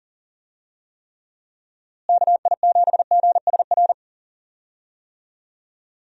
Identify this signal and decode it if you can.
Morse code, heard as a keyed tone, about 37 wpm, 700 Hz. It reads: KI7GSR
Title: USA Call Signs - 6 Characters - Repeated 3x - 40wpm